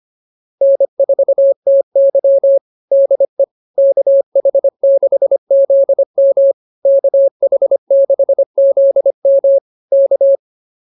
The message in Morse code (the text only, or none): N4TY DE KH6ZM KH6ZM K